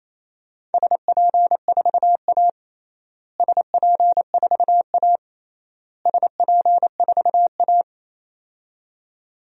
Morse code, heard as a keyed tone, about 28 wpm, 700 Hz. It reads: SP4A SP4A SP4A
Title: DXCC-100 Call Signs - 4 Characters - Repeated 3x - 28wpm